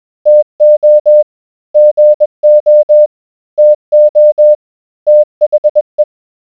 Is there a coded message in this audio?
Yes: Morse code from 0.3 to 6.0 s, heard as a keyed tone, about 21 wpm, 600 Hz. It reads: TO GO TO THE